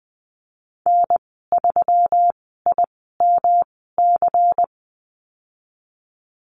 Morse code, heard as a keyed tone, about 20 wpm, spaced as for 14 wpm, 700 Hz. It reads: N3IMC
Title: 161 Lesson 26g - Letter g - Call Signs 20wpm